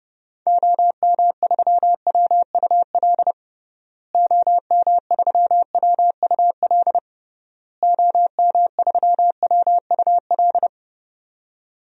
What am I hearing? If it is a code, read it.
Morse code, heard as a keyed tone, about 32 wpm, 700 Hz. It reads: OM3WUL OM3WUL OM3WUL
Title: DXCC-100 Call Signs - 6 Characters - Repeated 3x - 30wpm